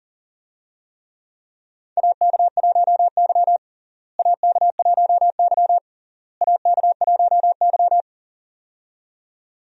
Morse code, heard as a keyed tone, about 40 wpm, 700 Hz. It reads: AK1Y AK1Y AK1Y
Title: USA Call Signs - 4 Characters - Repeated 3x - 40wpm